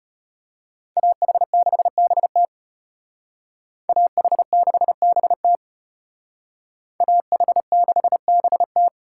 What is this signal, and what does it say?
Morse code, heard as a keyed tone, about 32 wpm, 700 Hz. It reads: AH6BT AH6BT AH6BT